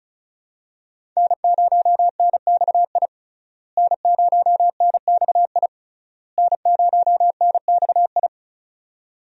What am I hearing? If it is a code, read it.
Morse code, heard as a keyed tone, about 36 wpm, 700 Hz. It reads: N0NXI N0NXI N0NXI
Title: USA Call Signs - 5 Characters - Repeated 3x - 35wpm